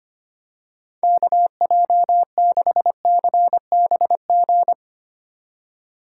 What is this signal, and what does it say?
Morse code, heard as a keyed tone, about 25 wpm, 700 Hz. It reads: KJ6CBG
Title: USA Call Signs - 2x Character Spacing - 25wpm